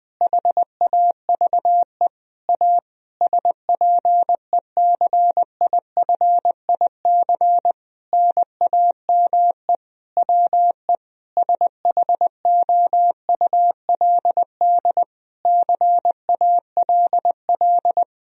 Morse code, heard as a keyed tone, about 20 wpm, 700 Hz. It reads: HAVE A SPECIFIC NAME WE SHOULD CALL
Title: Sentences from Top 1000 Words - Review Part 3 - 20wpm